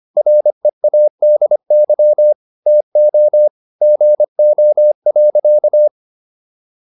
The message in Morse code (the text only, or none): READY TO GO.